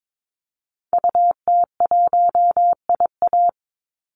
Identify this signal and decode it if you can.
Morse code, heard as a keyed tone, about 22 wpm, 700 Hz. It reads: UT1IA
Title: Call Signs - 22wpm